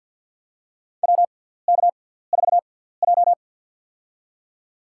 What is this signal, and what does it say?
Morse code, heard as a keyed tone, about 47 wpm, spaced as for 17 wpm, 690 Hz. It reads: WK4J
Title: USA Call Signs - 4 Characters - 6x Character Spacing - 50wpm